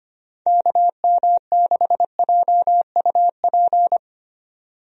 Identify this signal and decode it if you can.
Morse code, heard as a keyed tone, about 25 wpm, 700 Hz. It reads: KM6JUP
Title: USA Call Signs - 6 Characters - 25wpm